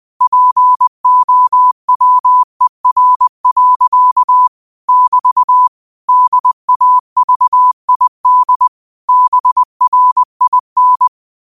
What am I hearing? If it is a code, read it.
Morse code, heard as a keyed tone, about 20 wpm, 1000 Hz. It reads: POWER. <BT> DAVID BRIN